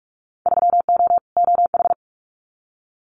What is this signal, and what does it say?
Morse code, heard as a keyed tone, about 46 wpm, 700 Hz. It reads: 3O OH
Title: Winter Field Day Exchanges 45wpm